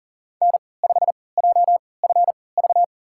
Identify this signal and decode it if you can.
Morse code, heard as a keyed tone, about 39 wpm, spaced as for 22 wpm, 700 Hz. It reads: N5JFV
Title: USA Call Signs - 5 Characters - 3x Character Spacing - 40wpm